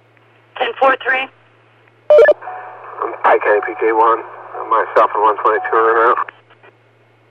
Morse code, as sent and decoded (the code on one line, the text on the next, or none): none